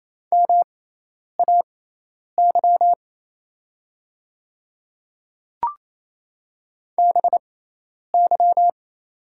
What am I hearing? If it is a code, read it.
Morse code, heard as a keyed tone, about 28 wpm, spaced as for 10 wpm, 700 Hz. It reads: MAY BY
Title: 6x Character Spacing - Top 100 Words - 28wpm